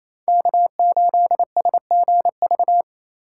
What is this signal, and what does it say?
Morse code, heard as a keyed tone, about 33 wpm, 700 Hz. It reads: K8SGV
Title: Speed Racing - USA 1x3 Call Signs 28wpm